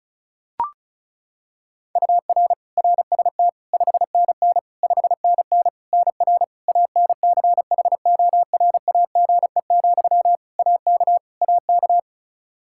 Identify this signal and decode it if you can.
Morse code, heard as a keyed tone, about 36 wpm, 700 Hz. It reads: UR RST 5NN 5NN NR ANCHORAGE, AK AK
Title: Practice QSO - Exchange 1 - Beginning - 35wpm